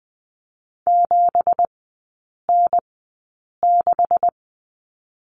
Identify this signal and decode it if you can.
Morse code, heard as a keyed tone, about 20 wpm, spaced as for 8 wpm, 700 Hz. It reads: 7N6
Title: 184 Lesson 30e - Letter b - Three Characters 20wpm